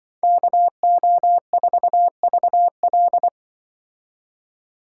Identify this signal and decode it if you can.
Morse code, heard as a keyed tone, about 25 wpm, 700 Hz. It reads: KO4VL
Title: Speed Racing - USA 2x2 Call Signs 20wpm